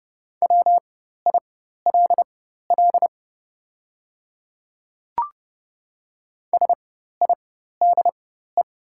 Morse code, heard as a keyed tone, about 30 wpm, spaced as for 14 wpm, 690 Hz. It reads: WILL SIDE
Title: 4x Character Spacing - Top 200 Words - 30wpm